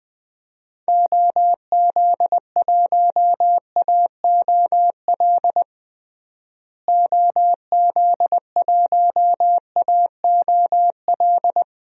Morse code, heard as a keyed tone, about 20 wpm, 700 Hz. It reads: OZ1AOL OZ1AOL
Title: DXCC-100 Call Signs - 6 Characters - Repeated 2x - 20wpm